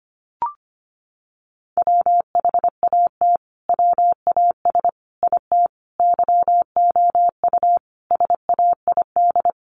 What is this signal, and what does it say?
Morse code, heard as a keyed tone, about 25 wpm, 700 Hz. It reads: WHAT WAS IT YOU SAID
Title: Sentences from Top 200 Words 25wpm